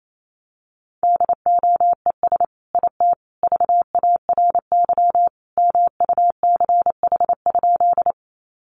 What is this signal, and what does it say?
Morse code, heard as a keyed tone, about 28 wpm, 700 Hz. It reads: DOES IT VARY MUCH?
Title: Sentences from Top 1200 Words 28wpm